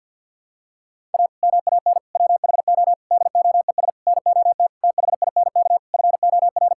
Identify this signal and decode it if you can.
Morse code, heard as a keyed tone, about 50 wpm, 700 Hz. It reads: A MAN WHO DOES NOT THINK FOR
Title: Sentences from Top 1600 Words 50wpm